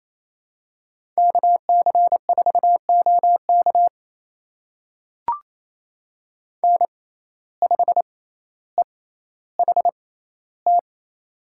Morse code, heard as a keyed tone, about 28 wpm, spaced as for 14 wpm, 700 Hz. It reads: KC4OK N5EHT
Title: USA Call Signs - 6x Character Spacing - 28wpm